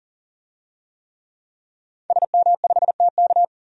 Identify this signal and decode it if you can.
Morse code, heard as a keyed tone, about 40 wpm, 700 Hz. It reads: SM5TK
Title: DXCC-100 Call Signs - 5 Characters - 40wpm